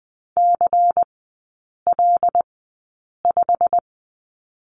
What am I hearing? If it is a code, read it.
Morse code, heard as a keyed tone, about 19 wpm, spaced as for 8 wpm, 700 Hz. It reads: CL5
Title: 138 Lesson 23e - Letter f - Three Characters 20wpm